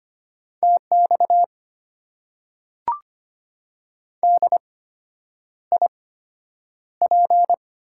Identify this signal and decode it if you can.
Morse code, heard as a keyed tone, about 25 wpm, spaced as for 9 wpm, 700 Hz. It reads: TX DIP
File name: Top 90 QSO Elements - 8x Character Spacing - 25wpm